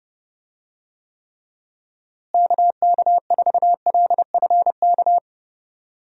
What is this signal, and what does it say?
Morse code, heard as a keyed tone, about 30 wpm, 700 Hz. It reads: KK4LFK
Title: USA Call Signs - 6 Characters - Repeated 3x - 30wpm